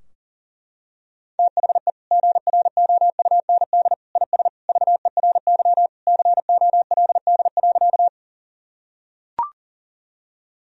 Morse code, heard as a keyed tone, about 40 wpm, 700 Hz. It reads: THE GROUND IS VERY COLD.
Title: Sentences from Top 1000 Words - Review Part 2 - 40wpm